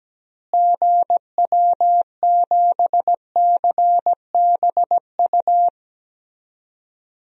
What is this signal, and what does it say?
Morse code, heard as a keyed tone, about 17 wpm, 700 Hz. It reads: GW7CBU
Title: DXCC-100 Call Signs - 6 Characters - Repeated 2x - 17wpm